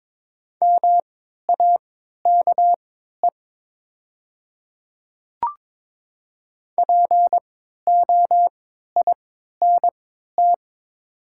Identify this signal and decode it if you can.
Morse code, heard as a keyed tone, about 22 wpm, spaced as for 13 wpm, 700 Hz. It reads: MAKE POINT
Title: 3x Character Spacing - Top 200 Words - 22wpm